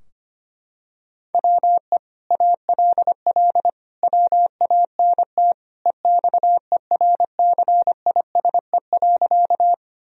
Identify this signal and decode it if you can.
Morse code, heard as a keyed tone, about 25 wpm, 700 Hz. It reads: WE ALL WANT EXERCISE.
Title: Sentences from Top 1000 Words - Review Part 4 - 25wpm